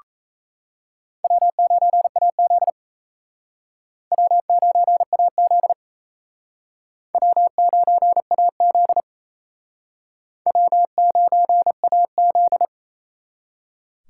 Morse code, heard as a keyed tone, about 34 wpm, 700 Hz. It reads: W9AZ W9AZ W9AZ W9AZ
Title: Speed Racing - USA 1x2 Call Signs 28wpm